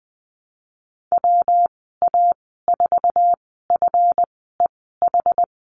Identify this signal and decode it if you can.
Morse code, heard as a keyed tone, about 19 wpm, spaced as for 14 wpm, 700 Hz. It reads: WA4FEH
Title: 154 Lesson 25g - Letter p - Call Signs 20wpm